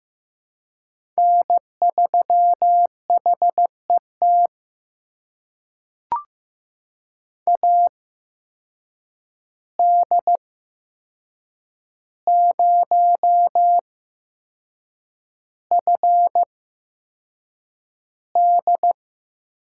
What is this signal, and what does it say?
Morse code, heard as a keyed tone, about 15 wpm, spaced as for 6 wpm, 700 Hz. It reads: N3HET AD0FD
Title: USA Call Signs - 8x Character Spacing - 15wpm